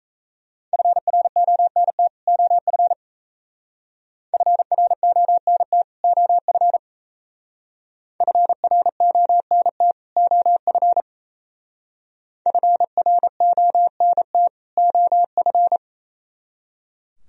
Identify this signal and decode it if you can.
Morse code, heard as a keyed tone, about 33 wpm, 690 Hz. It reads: FRONT OF FRONT OF FRONT OF FRONT OF
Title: Speed-Racing - Sets of 2 Words 28wpm